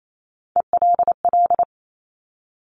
Morse code, heard as a keyed tone, about 28 wpm, 700 Hz. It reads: ELL
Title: Top Three Letter Combinations in Words 28wpm